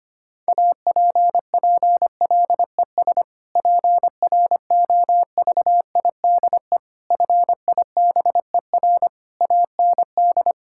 Morse code, heard as a keyed tone, about 25 wpm, 700 Hz. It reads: APPLES PROVIDE FIBER AND
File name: practice / 25wpm / Difficult Sentences with Hints - 25wpm